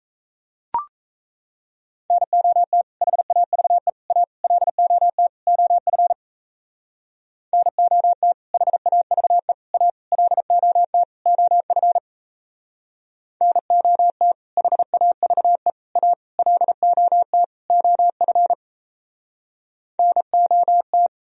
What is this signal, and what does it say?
Morse code, heard as a keyed tone, about 37 wpm, 700 Hz. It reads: NOT HAVE A LOT OF NOT HAVE A LOT OF NOT HAVE A LOT OF NOT